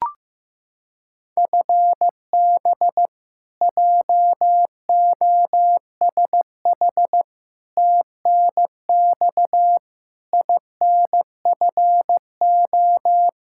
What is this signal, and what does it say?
Morse code, heard as a keyed tone, about 15 wpm, 700 Hz. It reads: FB JOSH TNX INFO
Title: Practice QSO - Exchange 3 - More Info - 15wpm